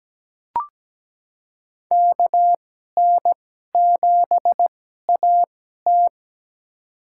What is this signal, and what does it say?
Morse code, heard as a keyed tone, about 17 wpm, spaced as for 12 wpm, 700 Hz. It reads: KN7AT
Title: USA Call Signs - 5 Characters - 2x Character Spacing - 17wpm